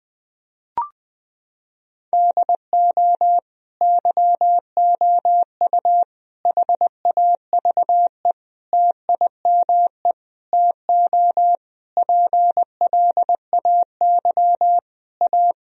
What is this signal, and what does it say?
Morse code, heard as a keyed tone, about 20 wpm, 700 Hz. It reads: DO YOU HAVE TIME TO PLAY A
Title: Sentences from Top 1000 Words - Review Part 4 - 20wpm